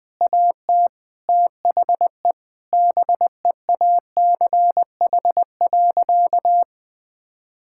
Morse code, heard as a keyed tone, about 20 wpm, 700 Hz. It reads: AT THE BEACH.
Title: Sentences with 6-Word Phrases - Spoken Beginning - 20wpm